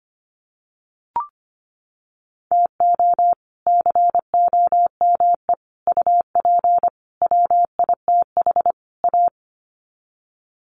Morse code, heard as a keyed tone, about 25 wpm, 700 Hz. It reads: TO COME UP WITH A
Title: Sets of 5 Words 25wpm